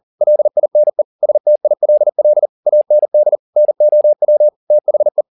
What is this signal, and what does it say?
Morse code, heard as a keyed tone, about 40 wpm, 600 Hz. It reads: LINE STILL AND NOW THE
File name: the old man and the sea 2 40wpm